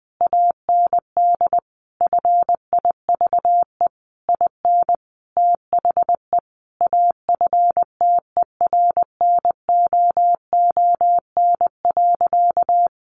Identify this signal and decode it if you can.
Morse code, heard as a keyed tone, about 20 wpm, 700 Hz. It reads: AND FIVE IN THE AFTERNOON.
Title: Sentences from Top 1500 Words 20wpm